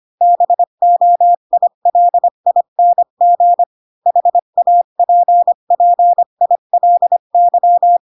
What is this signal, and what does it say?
Morse code, heard as a keyed tone, about 25 wpm, 700 Hz. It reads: BOILING HAPPILY
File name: Pinocchio / Pinocchio ch3 20 25 ik1hge